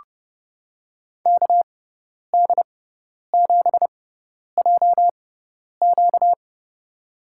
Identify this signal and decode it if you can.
Morse code, heard as a keyed tone, about 30 wpm, spaced as for 10 wpm, 700 Hz. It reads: KD7JQ
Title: USA Call Signs - 5 Characters - 6x Character Spacing - 30wpm